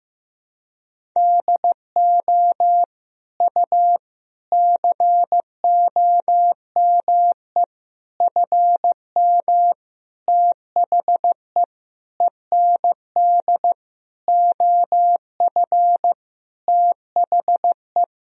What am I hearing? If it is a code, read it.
Morse code, heard as a keyed tone, about 15 wpm, 700 Hz. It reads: DO U COME FM THE END OF THE
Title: Sentences from Top 200 Words and Abbreviations 15wpm